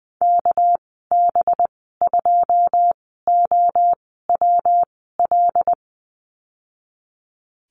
Morse code, heard as a keyed tone, about 20 wpm, spaced as for 14 wpm, 700 Hz. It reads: KB2OWL
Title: 200 Lesson 32g - Letter k - Call Signs 20wpm